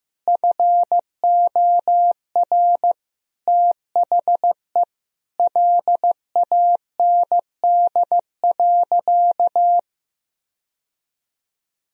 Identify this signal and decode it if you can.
Morse code, heard as a keyed tone, about 15 wpm, 700 Hz. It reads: FOR THE LAND.
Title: Sentences from Top 1000 Words - Review Part 1 - 15wpm